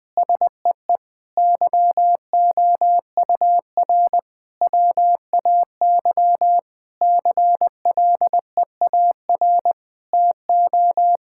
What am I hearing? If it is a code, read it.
Morse code, heard as a keyed tone, about 20 wpm, 700 Hz. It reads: SEE YOUR WAY CLEAR TO
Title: Sentences with 8-Word Phrases - 20wpm